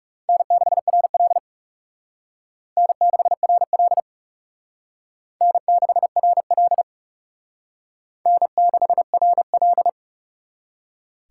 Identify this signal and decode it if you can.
Morse code, heard as a keyed tone, about 37 wpm, 700 Hz. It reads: N6RL N6RL N6RL N6RL
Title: Speed Racing - USA Call Signs 30wpm